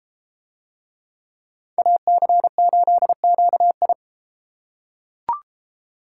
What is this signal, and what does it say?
Morse code, heard as a keyed tone, about 30 wpm, 700 Hz. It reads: AC8QI